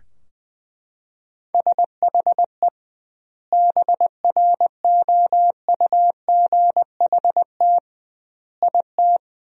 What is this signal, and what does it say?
Morse code, heard as a keyed tone, about 20 wpm, 700 Hz. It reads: SHE BROUGHT IT
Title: 2x Word-Spacing - Sentences from Top 500 Words - 20wpm